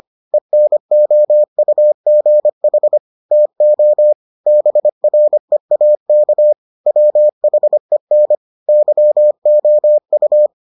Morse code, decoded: ENOUGH TO BREAK WHEN YOU

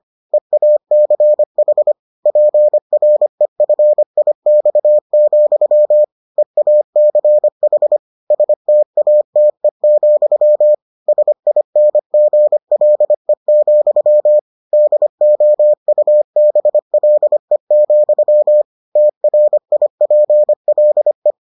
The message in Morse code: EACH PREFIX, EACH STATE, SINGLE, DOUBLE, TRIPLE

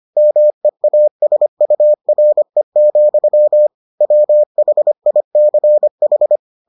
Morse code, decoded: MEASURE, WHICH

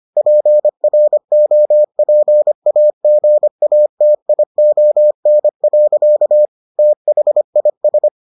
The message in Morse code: PROPAGATION. THIS

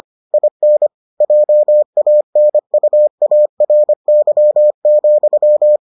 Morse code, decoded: IN JANUARY,

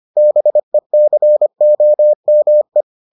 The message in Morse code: BECOME

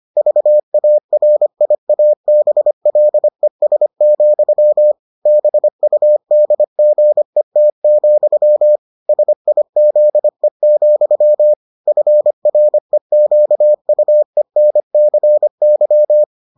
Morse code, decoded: VARIABLES, BUDGET, SIZE, FREQUENCY

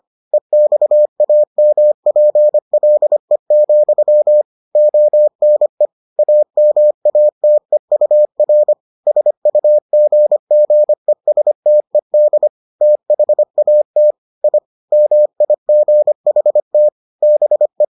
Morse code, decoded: EXAMPLE, ONE AMATEUR SUGGESTED THAT I MIGHT BE

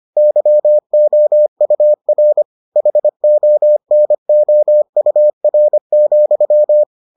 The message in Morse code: YOUR HONOUR,

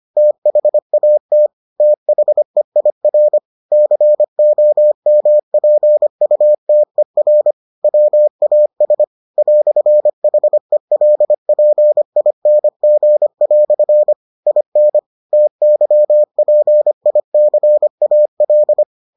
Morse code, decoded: THAT THEIR COMPUTER WAS "HELPING" IN TYPICAL

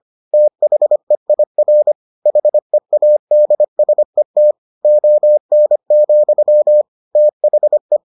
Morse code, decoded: THEIR HEADSET ON, THE

